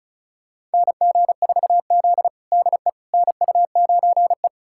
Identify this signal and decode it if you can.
Morse code, heard as a keyed tone, about 36 wpm, 700 Hz. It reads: NG4Z DE NU9E
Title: USA Callsign DE Callsign - 4 Characters - 35wpm